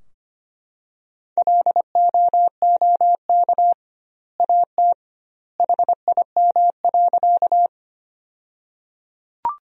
Morse code, heard as a keyed tone, about 25 wpm, 700 Hz. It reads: LOOK AT HIM.
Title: 2x Word-Spacing - Sentences from Top 100 Words - 25wpm